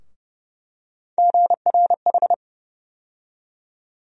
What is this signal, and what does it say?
Morse code, heard as a keyed tone, about 32 wpm, 700 Hz. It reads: GRH
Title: USA Call Sign Postfixes - 3 Letters 30wpm